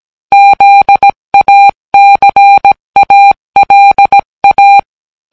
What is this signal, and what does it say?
Morse code, heard as a keyed tone, about 17 wpm, 800 Hz. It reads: ZACALA